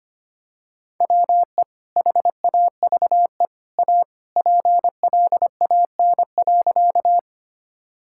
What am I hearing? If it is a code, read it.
Morse code, heard as a keyed tone, about 25 wpm, 700 Hz. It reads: WE HAVE A PLAN.
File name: practice / 25wpm / Sentences with 2-Word Phrases - 25wpm